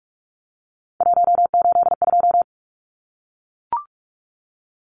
Morse code, heard as a keyed tone, about 44 wpm, 700 Hz. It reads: 182
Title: Numbers - 3-Digit 45wpm